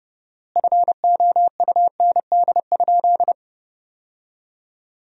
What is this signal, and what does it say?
Morse code, heard as a keyed tone, about 30 wpm, 700 Hz. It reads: FOUND?
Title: 4x Word-Spacing - Sentences from Top 300 Words - 30wpm